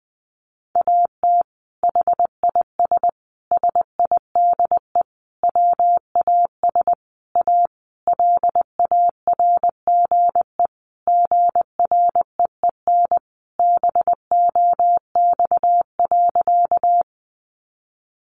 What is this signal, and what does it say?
Morse code, heard as a keyed tone, about 20 wpm, 700 Hz. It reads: AT HIS SIDE WAS A LARGE GREEN BOX.
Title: Sentences from Top 500 Words 20wpm